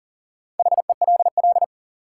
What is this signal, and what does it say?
Morse code, heard as a keyed tone, about 40 wpm, 700 Hz. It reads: HELL
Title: Sets of 5 Words Encore - New Word Drill - 40wpm